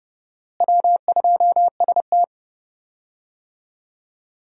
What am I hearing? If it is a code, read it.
Morse code, heard as a keyed tone, about 31 wpm, 700 Hz. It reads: W2ST
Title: USA Call Signs - 4 Characters - Repeated 2x - 30wpm